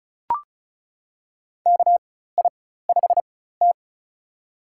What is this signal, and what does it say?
Morse code, heard as a keyed tone, about 34 wpm, spaced as for 16 wpm, 700 Hz. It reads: KI5T
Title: USA Call Signs - 4x Character Spacing - 35wpm